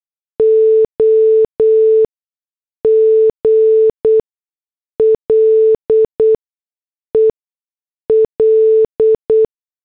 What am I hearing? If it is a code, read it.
Morse code, heard as a keyed tone, about 8 wpm, 440 Hz. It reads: O G L E L